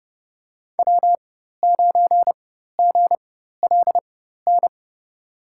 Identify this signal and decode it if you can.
Morse code, heard as a keyed tone, about 29 wpm, spaced as for 14 wpm, 700 Hz. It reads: W9GLN
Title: USA Call Signs - 5 Characters - 4x Character Spacing - 30wpm